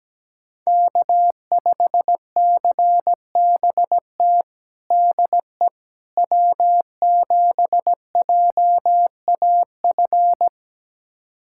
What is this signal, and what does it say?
Morse code, heard as a keyed tone, about 17 wpm, 700 Hz. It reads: K5CBT DE W7JAF
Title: USA Callsign DE Callsign - 5 Characters - 17wpm